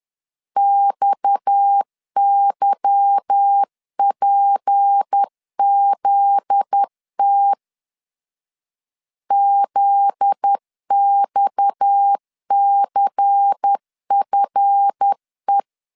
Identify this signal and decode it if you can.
Morse code, heard as a keyed tone, about 10 wpm, 800 Hz. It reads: XYPZT ZXCFE
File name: LEZIONI / 8 YJ45 / 8 YJ 10wpm 2651